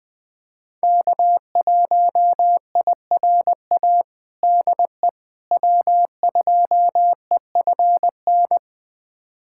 Morse code, heard as a keyed tone, about 20 wpm, 700 Hz. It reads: K1IRA DE W2EFN
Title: USA Callsign DE Callsign - 5 Characters - 20wpm